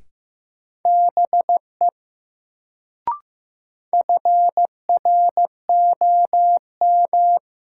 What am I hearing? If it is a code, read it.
Morse code, heard as a keyed tone, about 15 wpm, 700 Hz. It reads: BE FROM
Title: Sentences from Top 100 Words - New Word Drill - 15wpm